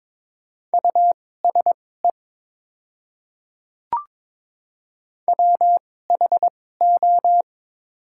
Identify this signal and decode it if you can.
Morse code, heard as a keyed tone, about 22 wpm, spaced as for 16 wpm, 700 Hz. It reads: USE WHO
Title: 2x Character Spacing - Top 100 Words - 22wpm